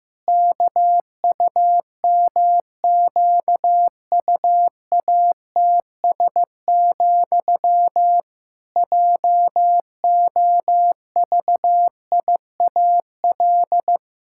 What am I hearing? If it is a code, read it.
Morse code, heard as a keyed tone, about 15 wpm, 700 Hz. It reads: KUMQUATS, JOVIAL
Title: Pangram Challenge v2 - 15wpm